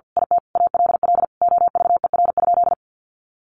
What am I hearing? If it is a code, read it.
Morse code, heard as a keyed tone, about 50 wpm, 700 Hz. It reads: IT ALL OVER?
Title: Sentences from Top 1200 Words 50wpm